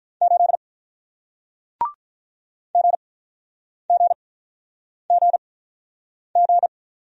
Morse code, heard as a keyed tone, about 46 wpm, spaced as for 9 wpm, 700 Hz. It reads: 8 GGGG